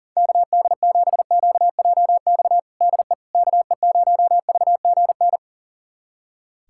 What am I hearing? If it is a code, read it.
Morse code, heard as a keyed tone, about 40 wpm, 700 Hz. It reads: KD7QJX DE KE0VGN
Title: USA Callsign DE Callsign 40wpm